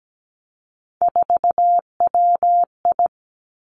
Morse code, heard as a keyed tone, about 17 wpm, 700 Hz. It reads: 4WI
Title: Three Challenging Letter-Number Combinations 17wpm